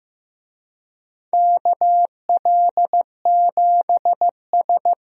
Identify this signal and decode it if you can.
Morse code, heard as a keyed tone, about 16 wpm, 700 Hz. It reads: KL7S
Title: DXCC-100 Call Signs - 4 Characters - Repeated 3x - 15wpm